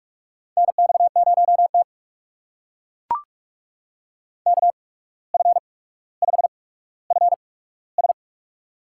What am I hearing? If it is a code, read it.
Morse code, heard as a keyed tone, about 44 wpm, spaced as for 18 wpm, 700 Hz. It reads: NX0T KF5FS
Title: USA Call Signs - 8x Character Spacing - 45wpm